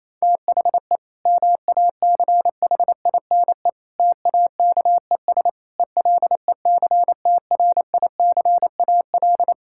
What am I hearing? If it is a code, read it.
Morse code, heard as a keyed tone, about 28 wpm, 700 Hz. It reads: THE MACHINE TAKES ELECTRICAL
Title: Sentences from Top 1000 Words - Review Part 6 - 28wpm